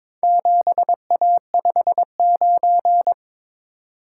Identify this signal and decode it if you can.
Morse code, heard as a keyed tone, about 23 wpm, 700 Hz. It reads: 7A59
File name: practice / 22wpm / Four Letter-Number Combinations - 22wpm